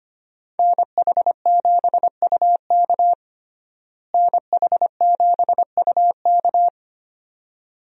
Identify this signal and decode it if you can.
Morse code, heard as a keyed tone, about 25 wpm, 700 Hz. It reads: NH7UK NH7UK
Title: USA Call Signs - 5 Characters - Repeated 2x - 25wpm